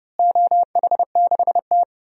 Morse code, heard as a keyed tone, about 30 wpm, 700 Hz. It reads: OH6T
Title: DXCC-100 Call Signs - 4 Characters - Repeated 3x - 30wpm